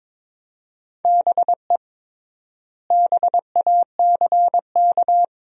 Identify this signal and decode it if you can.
Morse code, heard as a keyed tone, about 22 wpm, 700 Hz. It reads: BE BACK
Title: 3x Word-Spacing - Sentences from Top 500 Words - 22wpm